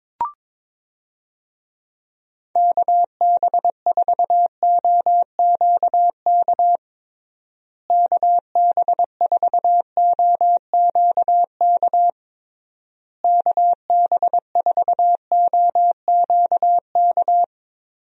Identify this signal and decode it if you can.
Morse code, heard as a keyed tone, about 22 wpm, 700 Hz. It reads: KB4OQK KB4OQK KB4OQK
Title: USA Call Signs - 6 Characters - Repeated 3x - 22wpm